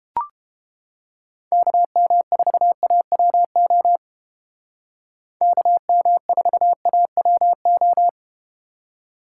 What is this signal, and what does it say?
Morse code, heard as a keyed tone, about 32 wpm, 700 Hz. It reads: KM4AWO KM4AWO